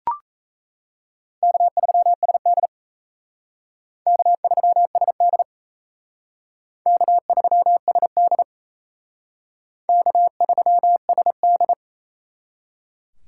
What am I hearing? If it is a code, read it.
Morse code, heard as a keyed tone, about 34 wpm, 700 Hz. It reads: K3SD K3SD K3SD K3SD
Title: Speed Racing - USA 1x2 Call Signs 28wpm